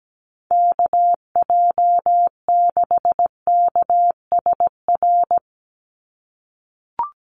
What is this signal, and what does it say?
Morse code, heard as a keyed tone, about 17 wpm, 700 Hz. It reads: KJ6KSR
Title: USA Call Signs 17wpm